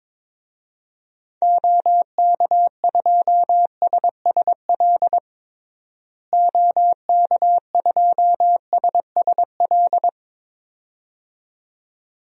Morse code, heard as a keyed tone, about 22 wpm, 700 Hz. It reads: OK2SSL OK2SSL
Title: DXCC-100 Call Signs - 6 Characters - Repeated 2x - 22wpm